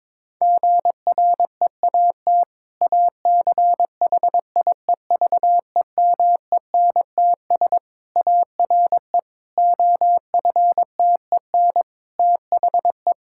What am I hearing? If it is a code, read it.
Morse code, heard as a keyed tone, about 22 wpm, 700 Hz. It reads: GREAT ACHIEVEMENTS ARE OFTEN THE
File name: Difficult Sentences with Hints - 22wpm